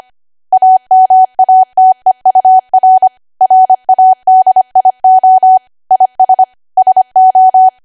Morse code, heard as a keyed tone, about 25 wpm, 750 Hz. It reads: AMATEUR RADIO IS SO
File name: W1AW-Level3/140218 25WPM